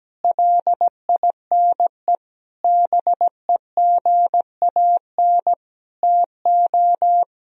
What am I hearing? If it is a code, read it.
Morse code, heard as a keyed tone, about 17 wpm, 700 Hz. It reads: LINE BEGAN TO
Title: Sentences from Top 1000 Words - Review Part 1 - 17wpm